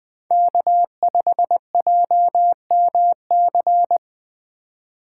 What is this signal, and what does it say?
Morse code, heard as a keyed tone, about 20 wpm, 700 Hz. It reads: K5JMC
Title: USA Call Signs - 5 Characters - 2x Character Spacing - 20wpm